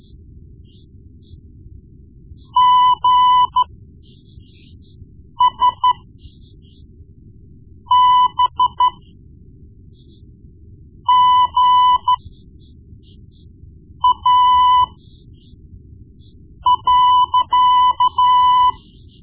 Morse code, heard as a keyed tone, about 13 wpm, spaced as for 4 wpm, 1000 Hz. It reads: GSBGA.
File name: Morse Lessons / Morse Lessons / Tape-1B